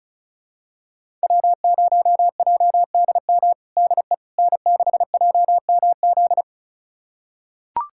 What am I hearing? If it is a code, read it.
Morse code, heard as a keyed tone, about 34 wpm, 700 Hz. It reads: W0JDM DE N6JMZ
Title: USA Callsign DE Callsign 35wpm